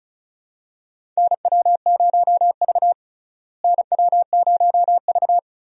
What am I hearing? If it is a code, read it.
Morse code, heard as a keyed tone, about 36 wpm, 700 Hz. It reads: NW0V NW0V
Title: USA Call Signs - 4 Characters - Repeated 2x - 35wpm